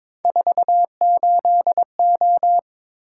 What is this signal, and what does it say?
Morse code, heard as a keyed tone, about 26 wpm, 700 Hz. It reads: 48O
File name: Speed-Racing - Three Letter-Number Combinations - 22wpm